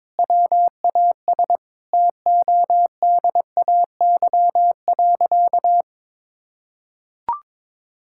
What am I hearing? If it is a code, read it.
Morse code, heard as a keyed tone, about 22 wpm, 700 Hz. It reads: WAS TODAY.
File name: practice / 22wpm / Sentences with 3-Word Phrases - 22wpm